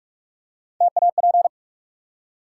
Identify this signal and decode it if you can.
Morse code, heard as a keyed tone, about 44 wpm, 700 Hz. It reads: TAP
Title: Three Letter Words - 45wpm